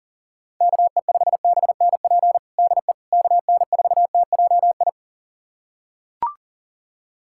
Morse code, heard as a keyed tone, about 40 wpm, 700 Hz. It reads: KE5BNP DE KN4TJI
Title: USA Callsign DE Callsign 40wpm